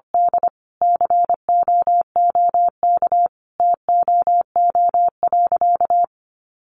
Morse code, heard as a keyed tone, about 24 wpm, 700 Hz. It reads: D COOK TOO.
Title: Sentences from Top 700 Words 25wpm